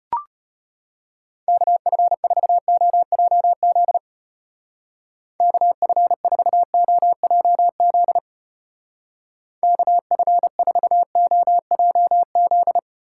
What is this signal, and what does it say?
Morse code, heard as a keyed tone, about 31 wpm, 700 Hz. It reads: KF4OJZ KF4OJZ KF4OJZ
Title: Speed Racing - USA Call Signs 25wpm